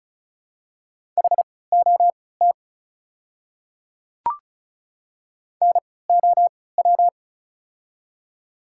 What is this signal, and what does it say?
Morse code, heard as a keyed tone, about 36 wpm, spaced as for 20 wpm, 690 Hz. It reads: HOT NOW
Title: 3x Character Spacing - Top 100 Words - 35wpm